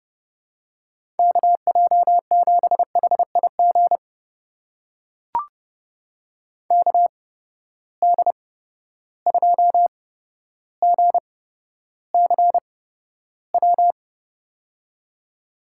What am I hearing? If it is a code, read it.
Morse code, heard as a keyed tone, about 32 wpm, spaced as for 13 wpm, 700 Hz. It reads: KJ7HIG KD2GCW
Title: USA Call Signs - 6 Characters - 8x Character Spacing - 30wpm